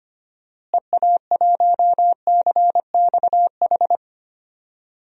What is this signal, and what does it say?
Morse code, heard as a keyed tone, about 25 wpm, 700 Hz. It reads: EA1CXH
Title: DXCC-100 Call Signs - 6 Characters - Repeated 3x - 25wpm